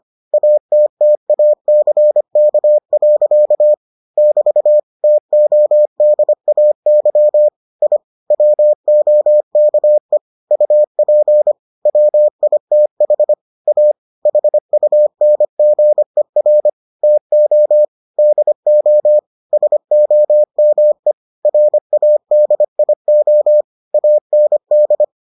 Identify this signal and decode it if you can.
Morse code, heard as a keyed tone, about 25 wpm, 600 Hz. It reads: ATTACK. <BT> TODAY I WOKE UP WITH A HUNGER TO DO SOME RADIO AND